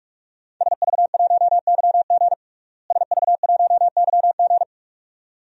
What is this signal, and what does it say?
Morse code, heard as a keyed tone, about 46 wpm, 700 Hz. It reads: SV1YG SV1YG
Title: DXCC-100 Call Signs - 5 Characters - Repeated 2x - 45wpm